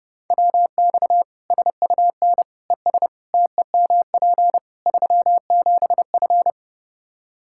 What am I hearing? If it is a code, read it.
Morse code, heard as a keyed tone, about 30 wpm, 700 Hz. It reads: WX SUN ES TEMP 37F
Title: K1USN SST QSO Practice 30wpm